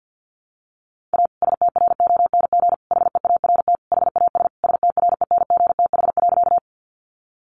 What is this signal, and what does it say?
Morse code, heard as a keyed tone, about 50 wpm, 700 Hz. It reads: A STRONG HEART HAS STRENGTH.
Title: Sentences from Top 1300 Words 50wpm